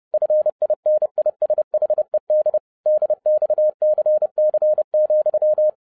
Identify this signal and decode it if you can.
Morse code, heard as a keyed tone, about 30 wpm, 610 Hz. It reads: FINISHED DXCC,
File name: 30wpm